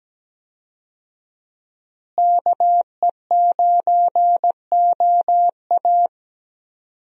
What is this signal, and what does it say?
Morse code, heard as a keyed tone, about 17 wpm, 700 Hz. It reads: KE9OA
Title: USA Call Signs - 5 Characters - Repeated 3x - 17wpm